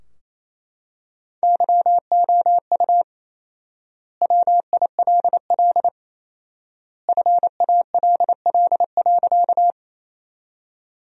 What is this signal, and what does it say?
Morse code, heard as a keyed tone, about 28 wpm, 700 Hz. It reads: YOU WILL FALL.
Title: 4x Word-Spacing - Sentences from Top 500 Words - 28wpm